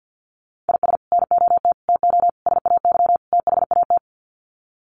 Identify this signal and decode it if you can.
Morse code, heard as a keyed tone, about 48 wpm, 700 Hz. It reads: IS NOT TO SAY THAT
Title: Sets of 5 Words 50wpm